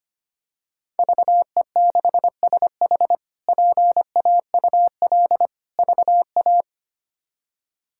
Morse code, heard as a keyed tone, about 25 wpm, 700 Hz. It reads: VE6SH PAUL VA
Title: K1USN SST QSO Practice - 25wpm